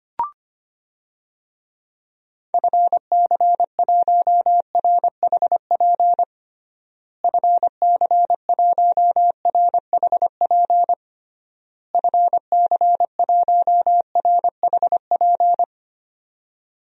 Morse code, heard as a keyed tone, about 25 wpm, 700 Hz. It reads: FC1RHP FC1RHP FC1RHP
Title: DXCC-100 Call Signs - 6 Characters - Repeated 3x - 25wpm